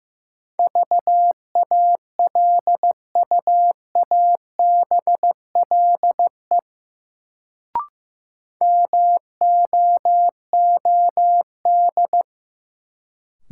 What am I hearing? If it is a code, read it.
Morse code, heard as a keyed tone, about 15 wpm, 700 Hz. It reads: VALUABLE MOOD
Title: Sentences from Top 1100 Words - New Word Drill - 15wpm